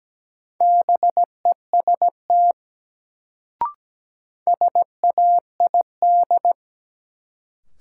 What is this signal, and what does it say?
Morse code, heard as a keyed tone, about 17 wpm, 700 Hz. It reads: BEST SAID
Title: Sets of 2 Words - New Word Drill - 17wpm